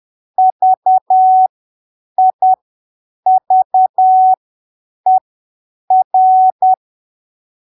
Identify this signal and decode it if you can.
Morse code, heard as a keyed tone, about 10 wpm, spaced as for 7 wpm, 750 Hz. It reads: VIVER
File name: arquivos / texto / texto 21 10ppm